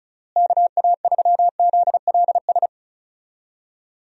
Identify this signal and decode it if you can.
Morse code, heard as a keyed tone, about 36 wpm, 700 Hz. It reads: KA3ZLS
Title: USA Call Signs - 6 Characters - Repeated 3x - 35wpm